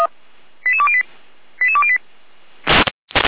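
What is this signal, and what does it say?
Five-tone selective call (ZVEI), heard as quick runs of five tones, tones of about 70 ms. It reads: 88208 88208
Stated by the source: ZVEI five-tone call received via SDR, fire brigade Orlamünde (feuerwehr-orlamuende).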